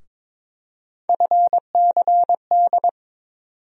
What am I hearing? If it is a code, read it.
Morse code, heard as a keyed tone, about 21 wpm, 700 Hz. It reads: FCD
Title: USA Call Sign Postfixes - 3 Letters 22wpm